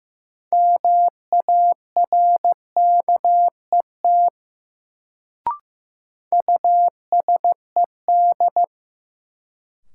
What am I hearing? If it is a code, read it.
Morse code, heard as a keyed tone, about 15 wpm, 700 Hz. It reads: MARKET USED
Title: Sentences from Top 600 Words - New Word Drill - 15wpm